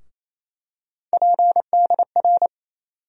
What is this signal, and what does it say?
Morse code, heard as a keyed tone, about 28 wpm, 700 Hz. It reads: PDR
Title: USA Call Sign Postfixes - 3 Letters 28wpm